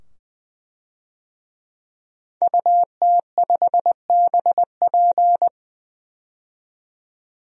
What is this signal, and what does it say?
Morse code, heard as a keyed tone, about 20 wpm, 700 Hz. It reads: UT5BP
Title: DXCC-100 Call Signs - 5 Characters - 20wpm